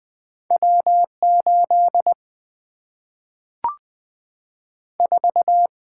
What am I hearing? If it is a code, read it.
Morse code, heard as a keyed tone, about 20 wpm, 700 Hz. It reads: W 8 4
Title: Two Letter-Number Combinations - 20wpm